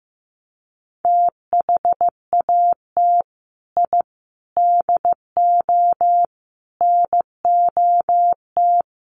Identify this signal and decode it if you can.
Morse code, heard as a keyed tone, about 15 wpm, 700 Hz. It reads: THAT I DO NOT
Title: Sets of 5 Words Encore2 15wpm